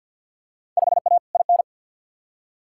Morse code, heard as a keyed tone, about 47 wpm, 700 Hz. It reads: 5A IN
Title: Field Day Exchanges Encore - 50wpm